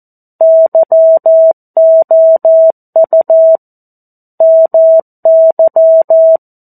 Morse code, heard as a keyed tone, about 14 wpm, 650 Hz. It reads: YOU MY
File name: 02-Chapter-02 14 WPM